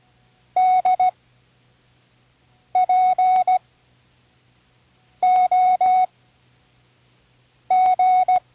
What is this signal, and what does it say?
Morse code, heard as a keyed tone, about 16 wpm, spaced as for 5 wpm, 740 Hz. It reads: DPOG
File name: CWResources / Learning Morse Code (from Tune in the World) / 19 random